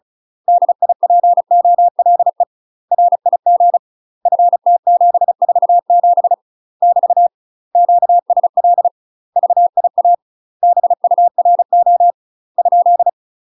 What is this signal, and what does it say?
Morse code, heard as a keyed tone, about 36 wpm, 700 Hz. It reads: DIPOLE RIG FT747 <BT> QSL VIA BURO ?
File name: UK MASTER QSOs TRAINING / 35WPM / UK MASTER QSOs 0023